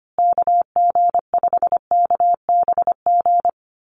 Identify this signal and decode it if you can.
Morse code, heard as a keyed tone, about 25 wpm, 700 Hz. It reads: KG5KBG
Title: USA Call Signs Repeated 3x 25wpm